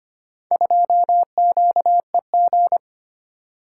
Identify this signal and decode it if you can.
Morse code, heard as a keyed tone, about 25 wpm, 700 Hz. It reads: 2QEG
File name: practice / 25wpm / Four Letter-Number Combinations - 25wpm